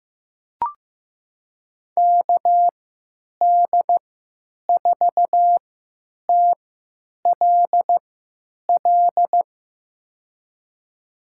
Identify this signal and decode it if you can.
Morse code, heard as a keyed tone, about 15 wpm, spaced as for 9 wpm, 700 Hz. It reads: KD4TLL
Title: USA Call Signs - 3x Character Spacing - 15wpm